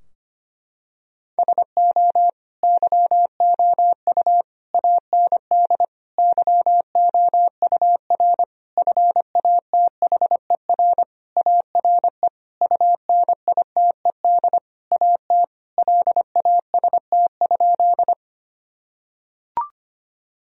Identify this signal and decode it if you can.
Morse code, heard as a keyed tone, about 25 wpm, 700 Hz. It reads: SO YOU AND YOUR FATHER ARE UNITED AT LAST?
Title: Sentences from Top 1000 Words - Review Part 4 - 25wpm